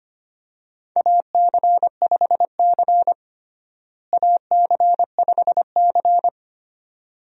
Morse code, heard as a keyed tone, about 25 wpm, 700 Hz. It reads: AC5C AC5C
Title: USA Call Signs - 4 Characters - Repeated 2x - 25wpm